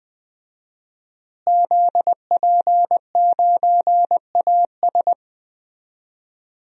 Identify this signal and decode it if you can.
Morse code, heard as a keyed tone, about 20 wpm, 700 Hz. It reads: ZP9AS
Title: DXCC-100 Call Signs - 5 Characters - Repeated 3x - 20wpm